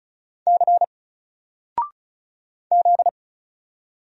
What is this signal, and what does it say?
Morse code, heard as a keyed tone, about 35 wpm, 700 Hz. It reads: C Z
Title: Single Character (Letter:Number:Punctuation:Prosign) 35wpm